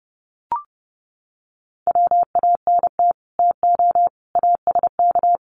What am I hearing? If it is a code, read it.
Morse code, heard as a keyed tone, about 29 wpm, 700 Hz. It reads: WANT TO ASK
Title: Sets of 5 Words Encore2 30wpm